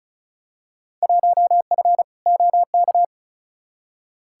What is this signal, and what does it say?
Morse code, heard as a keyed tone, about 36 wpm, 690 Hz. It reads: 1F OK
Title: Field Day Exchanges Encore - 35wpm